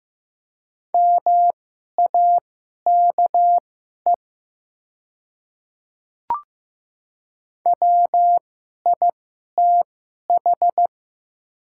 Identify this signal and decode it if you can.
Morse code, heard as a keyed tone, about 15 wpm, spaced as for 11 wpm, 700 Hz. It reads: MAKE WITH
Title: 2x Character Spacing - Top 100 Words - 15wpm